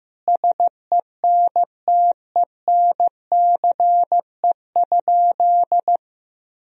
Morse code, heard as a keyed tone, about 15 wpm, 700 Hz. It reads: SENTENCE?
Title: Sentences from Top 1000 Words - Review Part 1 - 15wpm